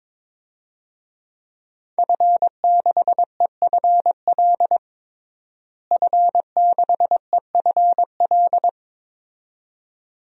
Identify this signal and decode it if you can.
Morse code, heard as a keyed tone, about 22 wpm, 700 Hz. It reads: F6EFL F6EFL
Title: DXCC-100 Call Signs - 5 Characters - Repeated 2x - 22wpm